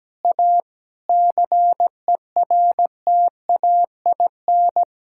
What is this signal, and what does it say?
Morse code, heard as a keyed tone, about 17 wpm, 700 Hz. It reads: A CERTAIN
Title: Sentences from Top 1000 Words - Review Part 6 - 17wpm